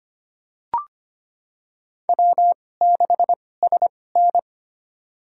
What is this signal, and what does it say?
Morse code, heard as a keyed tone, about 25 wpm, spaced as for 18 wpm, 700 Hz. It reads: W6SN
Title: USA Call Signs - 4 Characters - 2x Character Spacing - 25wpm